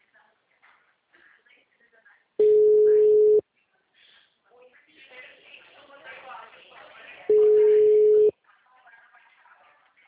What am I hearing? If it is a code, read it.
Ringback tone (Europe, 425 Hz), with speech.
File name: out- -2020- - - .368